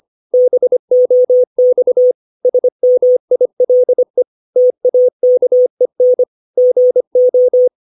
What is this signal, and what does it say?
Morse code, heard as a keyed tone, about 25 wpm, 500 Hz. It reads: BOX SMILE TAKEN GO
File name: 632 words / 632 Words 25WPM